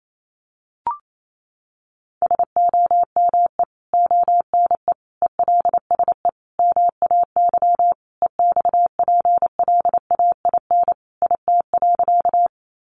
Morse code, heard as a keyed tone, about 28 wpm, 700 Hz. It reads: SOME ONE ELSE MAY EXPLAIN IT.
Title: Sentences from Top 1100 Words 28wpm